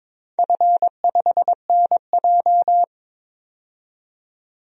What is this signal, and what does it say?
Morse code, heard as a keyed tone, about 22 wpm, 700 Hz. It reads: F5NJ
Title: DXCC-100 Call Signs - 4 Characters - 22wpm